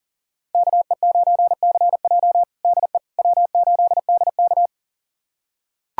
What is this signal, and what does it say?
Morse code, heard as a keyed tone, about 37 wpm, 700 Hz. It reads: KE9CJ DE W8DK
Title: USA Callsign DE Callsign 40wpm